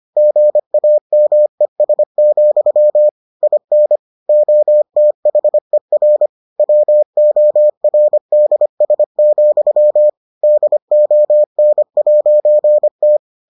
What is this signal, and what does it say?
Morse code, heard as a keyed tone, about 25 wpm, 600 Hz. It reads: GAMES, IN OTHER WORDS, DON'T